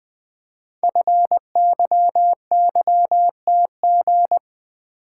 Morse code, heard as a keyed tone, about 20 wpm, 700 Hz. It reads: FYYTG
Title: Code Groups - 5 Letters - 20wpm